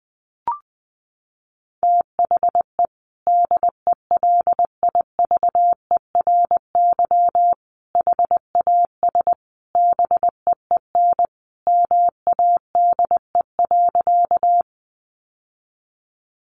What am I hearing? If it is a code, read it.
Morse code, heard as a keyed tone, about 20 wpm, 700 Hz. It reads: THE DELIVERY HAS BEEN MADE.
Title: Sentences from Top 1500 Words 20wpm